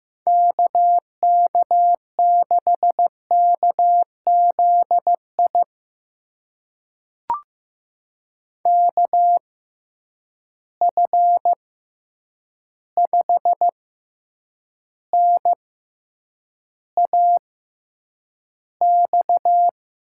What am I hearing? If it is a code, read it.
Morse code, heard as a keyed tone, about 15 wpm, spaced as for 8 wpm, 700 Hz. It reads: KK6KZI KF5NAX
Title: USA Call Signs - 6 Characters - 6x Character Spacing - 15wpm